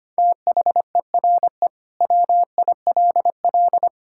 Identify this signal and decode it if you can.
Morse code, heard as a keyed tone, about 25 wpm, 700 Hz. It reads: THERE WILL
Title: Sentences from Top 1600 Words 25wpm